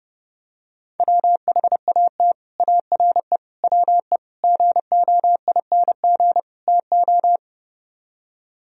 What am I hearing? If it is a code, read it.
Morse code, heard as a keyed tone, about 30 wpm, 700 Hz. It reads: WHAT ARE WE GOING TO